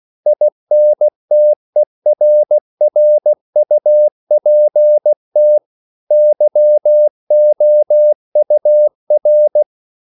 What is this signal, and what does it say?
Morse code, heard as a keyed tone, about 16 wpm, 600 Hz. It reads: INTERRUPT YOUR
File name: CW / Practice / EnglishQuotations0116